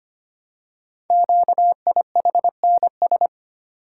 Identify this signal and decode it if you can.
Morse code, heard as a keyed tone, about 25 wpm, 700 Hz. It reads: QIHNS
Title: Code Groups - 5 Letters - 25wpm